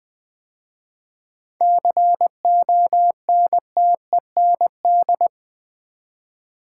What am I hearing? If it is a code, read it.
Morse code, heard as a keyed tone, about 20 wpm, 700 Hz. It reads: CONTEND
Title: 7 Letter Words - First 3 Letters Given - 20wpm